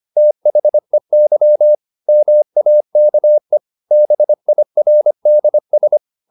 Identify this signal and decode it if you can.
Morse code, heard as a keyed tone, about 25 wpm, 600 Hz. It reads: THEY MAKE BIRDS
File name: the old man and the sea part 1 25wpm